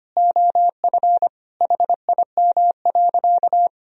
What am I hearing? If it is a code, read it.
Morse code, heard as a keyed tone, about 25 wpm, 700 Hz. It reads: OF HIM.
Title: Sentences from Top 1000 Words - Review Part 3 - 25wpm